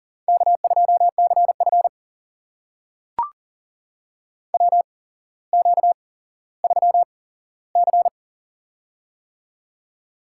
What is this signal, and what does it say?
Morse code, heard as a keyed tone, about 42 wpm, spaced as for 17 wpm, 700 Hz. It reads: K2CF WQ3C
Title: USA Call Signs - 4 Characters - 8x Character Spacing - 40wpm